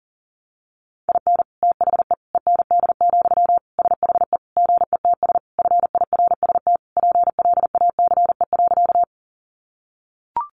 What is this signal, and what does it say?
Morse code, heard as a keyed tone, about 40 wpm, 700 Hz. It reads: IN THE END, SHE GETS FIRST PLACE.
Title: Additional Easy Phrases 40wpm